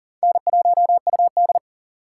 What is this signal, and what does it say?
Morse code, heard as a keyed tone, about 40 wpm, 700 Hz. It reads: N1UD
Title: USA Call Signs - 4 Characters - 4x Character Spacing - 40wpm